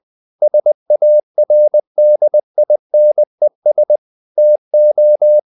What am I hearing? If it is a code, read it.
Morse code, heard as a keyed tone, about 20 wpm, 600 Hz. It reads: SARDINES TO
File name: the old man and the sea / the old man and the sea part 1 20wpm